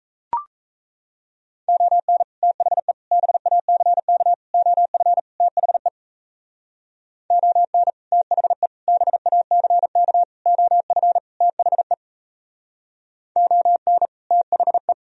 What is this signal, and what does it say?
Morse code, heard as a keyed tone, about 35 wpm, 700 Hz. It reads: ON THE BACK OF THE ON THE BACK OF THE ON THE